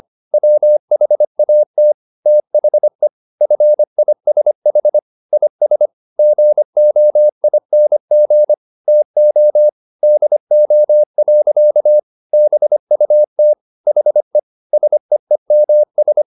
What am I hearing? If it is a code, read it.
Morse code, heard as a keyed tone, about 25 wpm, 600 Hz. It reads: WHAT THE FISH IS GOING TO DO. BUT HE SEEMS
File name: the old man and the sea part 1 25wpm